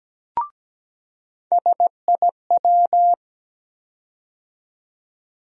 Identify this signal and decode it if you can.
Morse code, heard as a keyed tone, about 17 wpm, 700 Hz. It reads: SIW
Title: USA Call Sign Postfixes - 3 Letters 17wpm